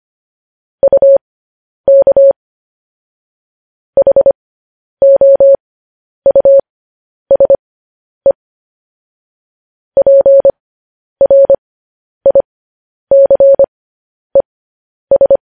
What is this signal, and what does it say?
Morse code, heard as a keyed tone, about 25 wpm, spaced as for 10 wpm, 550 Hz. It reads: UK HOUSE PRICES